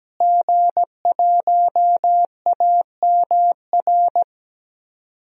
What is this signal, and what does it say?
Morse code, heard as a keyed tone, about 17 wpm, 700 Hz. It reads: G1AMR
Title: DXCC-100 Call Signs - 5 Characters - Repeated 3x - 17wpm